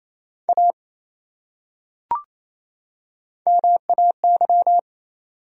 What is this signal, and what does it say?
Morse code, heard as a keyed tone, about 28 wpm, 700 Hz. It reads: A MAY
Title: Top 100 Words 28wpm